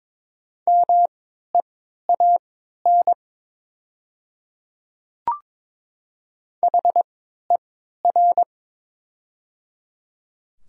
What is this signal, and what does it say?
Morse code, heard as a keyed tone, about 22 wpm, spaced as for 13 wpm, 700 Hz. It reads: MEAN HER
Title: 3x Character Spacing - Top 200 Words - 22wpm